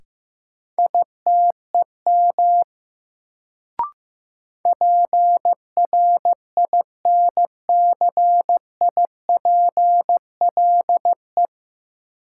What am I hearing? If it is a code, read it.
Morse code, heard as a keyed tone, about 15 wpm, 700 Hz. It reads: ITEM PRINCIPLE
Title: Sentences from Top 1400 Words - New Word Drill Alt - 15wpm